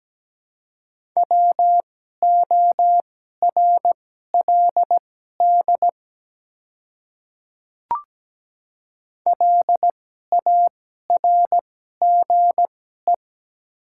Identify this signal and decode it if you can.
Morse code, heard as a keyed tone, about 17 wpm, spaced as for 12 wpm, 700 Hz. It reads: WORLD LARGE
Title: 2x Character Spacing - Top 300 Words - 17wpm